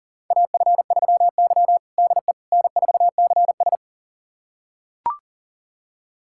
Morse code, heard as a keyed tone, about 37 wpm, 700 Hz. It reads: AF3Y DE N4CS
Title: USA Callsign DE Callsign - 4 Characters - 40wpm